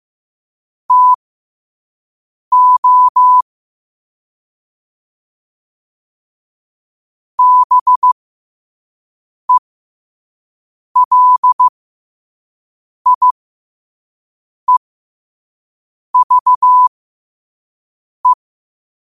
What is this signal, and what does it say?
Morse code, heard as a keyed tone, about 15 wpm, spaced as for 5 wpm, 1000 Hz. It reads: TO BELIEVE
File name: qotdfiles / 5wpm